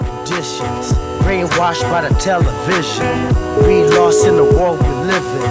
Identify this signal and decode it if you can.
Ringback tone (Europe, 425 Hz), with speech.